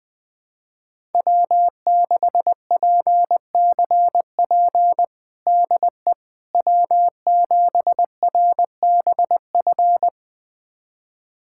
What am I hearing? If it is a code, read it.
Morse code, heard as a keyed tone, about 20 wpm, 700 Hz. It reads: W6PCP DE W7RBF
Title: USA Callsign DE Callsign - 5 Characters - 20wpm